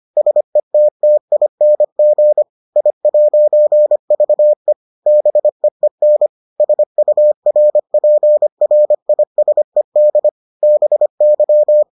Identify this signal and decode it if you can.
Morse code, heard as a keyed tone, about 25 wpm, 600 Hz. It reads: SETTING I'VE BEEN SURPRISED BY